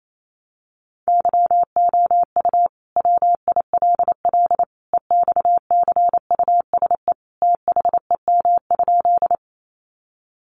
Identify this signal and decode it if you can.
Morse code, heard as a keyed tone, about 28 wpm, 700 Hz. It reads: YOU WILL EXCUSE THEM?
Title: Sentences from Top 700 Words 28wpm